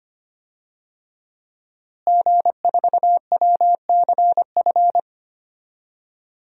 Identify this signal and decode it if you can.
Morse code, heard as a keyed tone, about 25 wpm, 700 Hz. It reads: G4WCF
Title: DXCC-100 Call Signs - 5 Characters - Repeated 3x - 25wpm